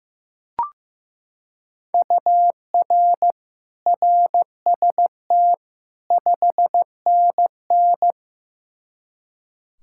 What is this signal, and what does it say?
Morse code, heard as a keyed tone, about 16 wpm, 700 Hz. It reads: UR RST 5NN
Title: K1USN SST QSO Practice 15wpm